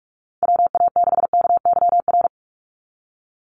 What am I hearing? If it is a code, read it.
Morse code, heard as a keyed tone, about 44 wpm, 700 Hz. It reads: WA6KYR
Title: NA SOTA Chaser Callsigns 45wpm